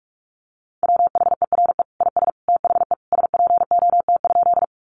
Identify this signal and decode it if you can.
Morse code, heard as a keyed tone, about 44 wpm, 700 Hz. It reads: WHERE IS THE SPOT?
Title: Sentences from Top 1100 Words 45wpm